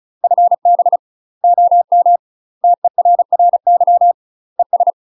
Morse code, heard as a keyed tone, about 34 wpm, 700 Hz. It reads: FB OM TERRY ES
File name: UK MASTER QSOs TRAINING / 35WPM / UK MASTER QSOs 0037